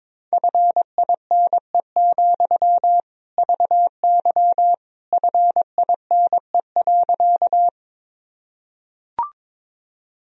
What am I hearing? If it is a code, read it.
Morse code, heard as a keyed tone, about 22 wpm, 700 Hz. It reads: FINE, VY FINE.
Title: Sentences from Top 500 Words and Abbreviations 22wpm